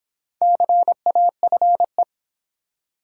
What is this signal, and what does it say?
Morse code, heard as a keyed tone, about 26 wpm, 700 Hz. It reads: CAFE